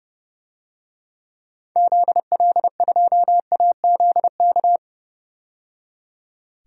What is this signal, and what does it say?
Morse code, heard as a keyed tone, about 30 wpm, 700 Hz. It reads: ZL2AZK
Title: DXCC-100 Call Signs - 6 Characters - 30wpm